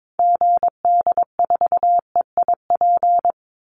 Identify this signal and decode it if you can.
Morse code, heard as a keyed tone, about 22 wpm, 690 Hz. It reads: GD4EIP
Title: Call Signs Repeated 3x - 22wpm